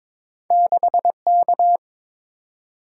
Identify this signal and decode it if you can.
Morse code, heard as a keyed tone, about 22 wpm, 700 Hz. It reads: 6K
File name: practice / 22wpm / Two Letter-Number Combinations - 22wpm